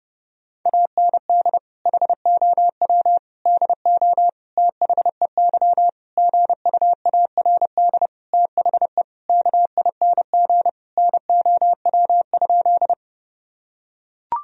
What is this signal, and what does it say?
Morse code, heard as a keyed tone, about 32 wpm, 700 Hz. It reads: AND HOW DO THEY GUARD THE KING NOW?
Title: Sentences from Top 1000 Words - Review Part 4 - 30wpm